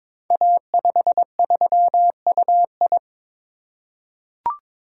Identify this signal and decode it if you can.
Morse code, heard as a keyed tone, about 22 wpm, 700 Hz. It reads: A53UI
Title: Code Groups - 5 Letter:Numbers - 22wpm